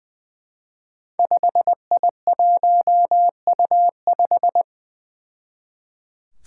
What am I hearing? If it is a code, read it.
Morse code, heard as a keyed tone, about 20 wpm, 700 Hz. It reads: 5I1U5
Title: Five Challenging Letter-Number Combinations 20wpm